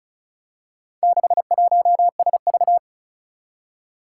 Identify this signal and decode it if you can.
Morse code, heard as a keyed tone, about 36 wpm, 700 Hz. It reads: 61SV
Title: Four Challenging Letter-Number Combinations 35wpm